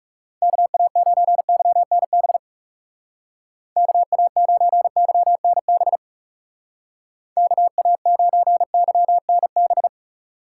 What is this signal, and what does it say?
Morse code, heard as a keyed tone, about 39 wpm, 700 Hz. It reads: KA9YNB KA9YNB KA9YNB